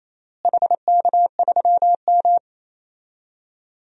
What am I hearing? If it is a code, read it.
Morse code, heard as a keyed tone, about 28 wpm, 700 Hz. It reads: HK3M
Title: DXCC-100 Call Signs - 4 Characters - Repeated 3x - 28wpm